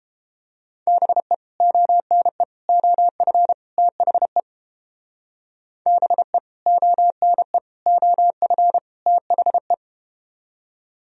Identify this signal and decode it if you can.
Morse code, heard as a keyed tone, about 31 wpm, 700 Hz. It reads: BE ONE OF THE BE ONE OF THE